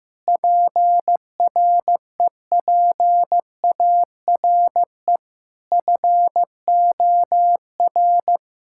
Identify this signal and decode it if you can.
Morse code, heard as a keyed tone, about 15 wpm, 700 Hz. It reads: PREPARE FOR
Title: Sentences from Top 1600 Words 15wpm